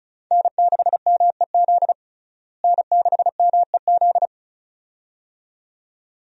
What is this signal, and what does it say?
Morse code, heard as a keyed tone, about 34 wpm, 700 Hz. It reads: N6MEZ N6MEZ
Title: USA Call Signs - 5 Characters - Repeated 2x - 35wpm